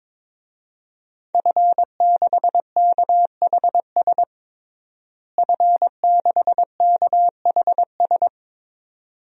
Morse code, heard as a keyed tone, about 22 wpm, 700 Hz. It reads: F6KHS F6KHS
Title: DXCC-100 Call Signs - 5 Characters - Repeated 2x - 22wpm